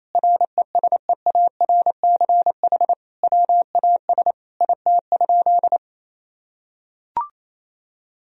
Morse code, heard as a keyed tone, about 28 wpm, 700 Hz. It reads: RESEARCH WAS IT?
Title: Sentences from Top 1000 Words - Review Part 4 - 28wpm